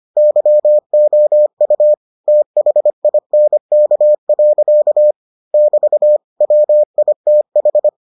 Morse code, heard as a keyed tone, about 25 wpm, 600 Hz. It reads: YOU THINK. <BT> WITH